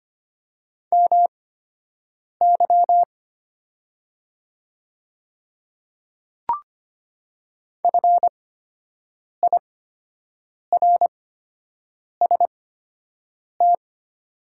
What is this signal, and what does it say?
Morse code, heard as a keyed tone, about 25 wpm, spaced as for 7 wpm, 700 Hz. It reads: MY FIRST
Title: 8x Character Spacing - Top 100 Words - 25wpm